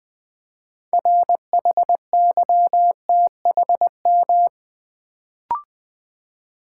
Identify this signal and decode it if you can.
Morse code, heard as a keyed tone, about 20 wpm, 700 Hz. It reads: RHYTHM
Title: Build A Word - 3 Letters At A Time - 6-Letter Words - 20wpm